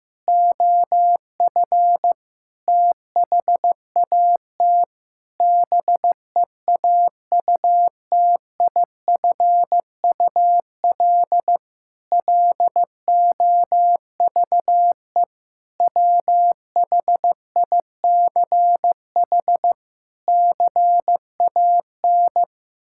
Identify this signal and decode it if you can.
Morse code, heard as a keyed tone, about 16 wpm, 700 Hz. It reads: OF THAT BEAUTIFUL LOVE WHICH CAN
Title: Sentences from Top 1600 Words 15wpm